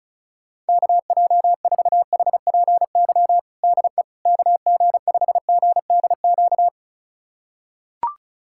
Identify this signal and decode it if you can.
Morse code, heard as a keyed tone, about 34 wpm, 690 Hz. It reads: KJ4HPY DE KG5GDQ
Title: USA Callsign DE Callsign 35wpm